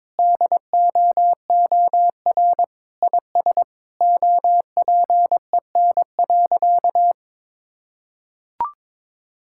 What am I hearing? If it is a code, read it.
Morse code, heard as a keyed tone, about 22 wpm, 700 Hz. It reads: DOOR IS OPEN.
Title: Sentences from Top 1000 Words - Review Part 6 - 22wpm